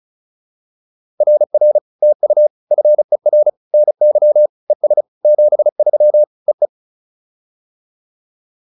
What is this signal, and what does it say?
Morse code, heard as a keyed tone, about 34 wpm, 600 Hz. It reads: RR TU FER NY ES 73 EE
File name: POTA206 35